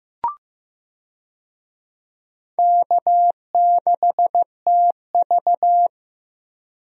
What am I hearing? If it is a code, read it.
Morse code, heard as a keyed tone, about 15 wpm, 700 Hz. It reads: K6TV
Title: USA Call Signs - 4 Characters - Repeated 3x - 15wpm